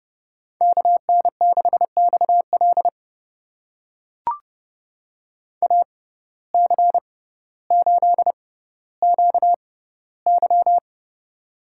Morse code, heard as a keyed tone, about 29 wpm, spaced as for 15 wpm, 700 Hz. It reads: KN6XL AC8QY
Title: USA Call Signs - 5 Characters - 6x Character Spacing - 30wpm